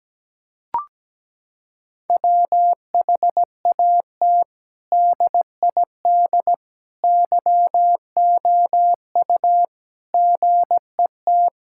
Morse code, heard as a keyed tone, about 17 wpm, 700 Hz. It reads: WHAT DID YOU GET
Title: Sentences from Top 1000 Words - Review Part 6 - 17wpm